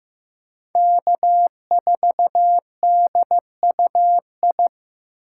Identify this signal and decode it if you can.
Morse code, heard as a keyed tone, about 18 wpm, 700 Hz. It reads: K4DUI
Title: Speed Racing - USA Call Signs 15wpm